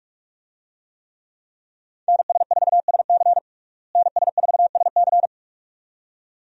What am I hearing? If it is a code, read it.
Morse code, heard as a keyed tone, about 46 wpm, 700 Hz. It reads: NS4SC NS4SC
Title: USA Call Signs - 5 Characters - Repeated 2x - 45wpm